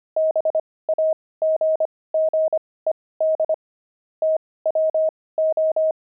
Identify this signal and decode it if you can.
Morse code, heard as a keyed tone, about 25 wpm, spaced as for 18 wpm, 630 Hz. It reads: BAGGED TWO
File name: pangram-fs-18-cs-25-aardvark